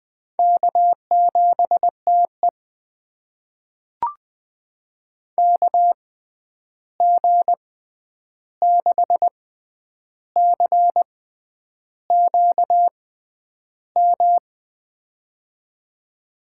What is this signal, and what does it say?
Morse code, heard as a keyed tone, about 20 wpm, spaced as for 9 wpm, 700 Hz. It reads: K7TE KG6CQM
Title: USA Call Signs - 6x Character Spacing - 20wpm